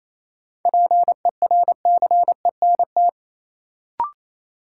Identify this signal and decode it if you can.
Morse code, heard as a keyed tone, about 28 wpm, 700 Hz. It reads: PERCENT
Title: Sets of 2 Words - New Word Drill - 28wpm